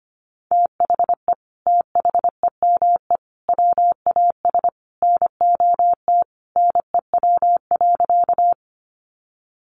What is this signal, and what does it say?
Morse code, heard as a keyed tone, about 25 wpm, 700 Hz. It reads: THE THEME WAS NOT NEW.
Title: Sentences from Top 1100 Words 25wpm